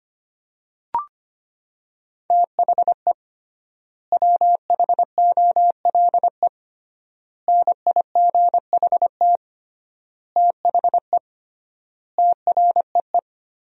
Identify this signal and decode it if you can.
Morse code, heard as a keyed tone, about 25 wpm, 700 Hz. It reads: THE WHOLE NIGHT THE TREE
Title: 3x Word-Spacing - Sentences from Top 500 Words - 25wpm